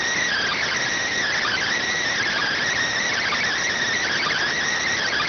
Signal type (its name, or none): Emergency Alert System (SAME) tones